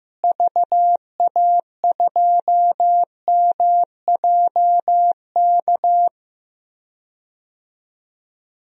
Morse code, heard as a keyed tone, about 15 wpm, 700 Hz. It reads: VA2MJK
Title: DXCC-100 Call Signs - 6 Characters - Repeated 3x - 15wpm